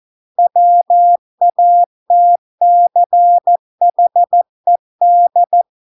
Morse code, heard as a keyed tone, about 14 wpm, 700 Hz. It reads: WATCHED